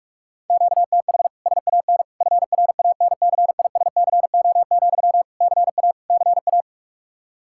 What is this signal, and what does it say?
Morse code, heard as a keyed tone, about 41 wpm, 700 Hz. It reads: QTH SAN FRANCISCO, CA CA
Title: K1USN SST QSO Practice 45wpm